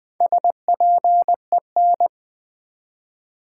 Morse code, heard as a keyed tone, about 20 wpm, 700 Hz. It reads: SPEN
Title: Build A Word - 1 Letter At A Time - 3 to 5-Letter Words - 20wpm